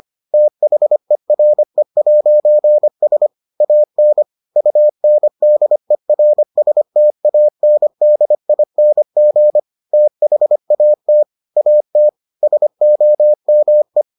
Morse code, heard as a keyed tone, about 25 wpm, 600 Hz. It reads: THERE'S AN UNDERSTANDING THAT AT SOME